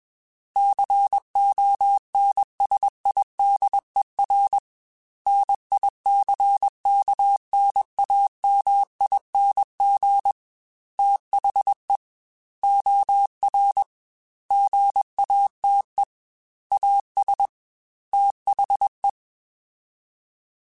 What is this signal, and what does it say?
Morse code, heard as a keyed tone, about 21 wpm, 800 Hz. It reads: CONSIDER NICKNAMING THE OR GATE AS THE
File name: Technical Subjects / Technical Subjects-mp3s / OR Gate @20